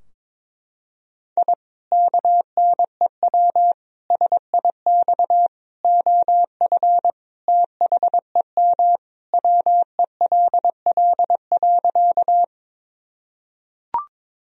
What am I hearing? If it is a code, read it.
Morse code, heard as a keyed tone, about 22 wpm, 700 Hz. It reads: I KNEW SIX OF THEM WELL.
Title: Sentences from Top 1000 Words - Review Part 2 - 22wpm